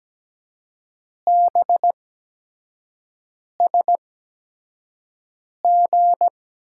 Morse code, heard as a keyed tone, about 17 wpm, spaced as for 5 wpm, 700 Hz. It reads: BSG
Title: USA Call Signs - 6 Characters - 8x Character Spacing - 17wpm